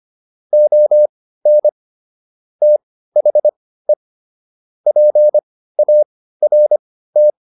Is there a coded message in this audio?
Yes: Morse code from 0.5 to 7.3 s, heard as a keyed tone, about 25 wpm, spaced as for 15 wpm, 600 Hz. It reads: ON THE PART